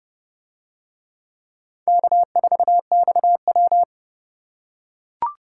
Morse code, heard as a keyed tone, about 29 wpm, 700 Hz. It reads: K4XW